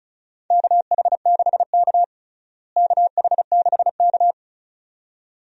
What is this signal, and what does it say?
Morse code, heard as a keyed tone, about 34 wpm, 700 Hz. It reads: KH6K KH6K
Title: DXCC-100 Call Signs - 4 Characters - Repeated 2x - 35wpm